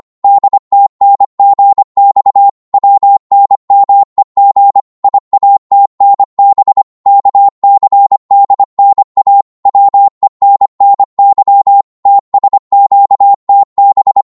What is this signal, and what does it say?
Morse code, heard as a keyed tone, about 25 wpm, 800 Hz. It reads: DTNGX WNMEG IATNB KCDNA WENNY TSQTB
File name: Morse / ltr-25wpm